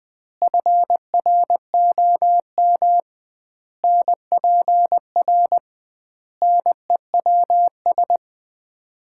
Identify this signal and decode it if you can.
Morse code, heard as a keyed tone, about 20 wpm, 700 Hz. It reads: FROM NPR NEWS
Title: 2x Word-Spacing - Sets of 5 Words - 20wpm